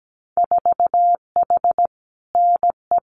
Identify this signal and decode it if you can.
Morse code, heard as a keyed tone, about 17 wpm, 700 Hz. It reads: 4H NE
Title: Winter Field Day Exchanges Encore 17wpm